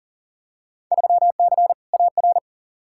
Morse code, heard as a keyed tone, about 39 wpm, 700 Hz. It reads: 3C AR
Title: Field Day Exchanges Encore - 40wpm